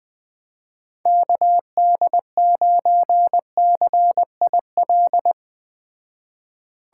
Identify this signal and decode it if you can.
Morse code, heard as a keyed tone, about 20 wpm, 700 Hz. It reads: KD9CIL
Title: Speed Racing - USA 2x3 Call Signs 17wpm